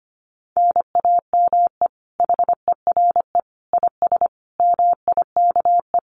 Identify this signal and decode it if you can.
Morse code, heard as a keyed tone, about 25 wpm, 700 Hz. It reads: NAME HERE IS MIKE
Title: Common QSO Phrases 25wpm